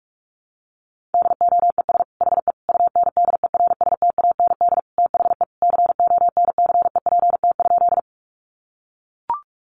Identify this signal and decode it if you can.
Morse code, heard as a keyed tone, about 44 wpm, 700 Hz. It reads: DOES HE UNDERSTAND THE CONCEPT?
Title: Sentences from Top 1100 Words 45wpm